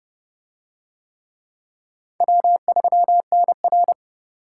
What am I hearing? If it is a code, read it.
Morse code, heard as a keyed tone, about 28 wpm, 700 Hz. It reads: W3NR